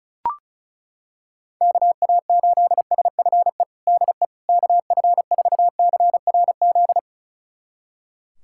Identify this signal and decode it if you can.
Morse code, heard as a keyed tone, about 36 wpm, 700 Hz. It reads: KA8SFE DE KF4CRZ
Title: USA Callsign DE Callsign - 6 Characters - 35wpm